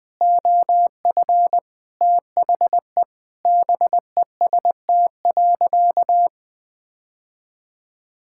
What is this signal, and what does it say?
Morse code, heard as a keyed tone, about 20 wpm, 700 Hz. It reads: OF THE BEST.
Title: Sentences with 7-Word Phrases - 20wpm